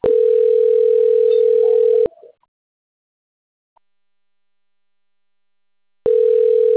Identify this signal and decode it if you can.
Ringback tone (North America), 440 + 480 Hz.
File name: rg-1600-unknown- - - .48554